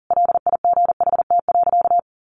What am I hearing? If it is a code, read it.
Morse code, heard as a keyed tone, about 40 wpm, 700 Hz. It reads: LIGHT.
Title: Sentences from Top 400 Words 40wpm